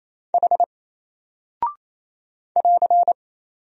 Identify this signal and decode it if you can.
Morse code, heard as a keyed tone, about 27 wpm, 700 Hz. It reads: H <AR>
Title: Single Character (Letter:Number:Punctuation:Prosign) 28wpm